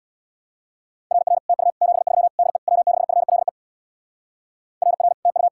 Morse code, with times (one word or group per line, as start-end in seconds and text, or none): 1.1-1.4 RR
1.5-1.7 TU
1.8-2.3 73
2.4-2.6 DE
2.7-3.5 W6FLE
4.8-5.1 RR
5.3-5.5 TU